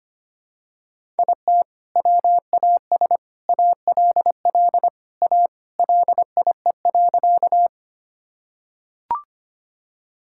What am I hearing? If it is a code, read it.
Morse code, heard as a keyed tone, about 25 wpm, 700 Hz. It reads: IT WAS ALL A LIE.
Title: Sentences from Top 1600 Words 25wpm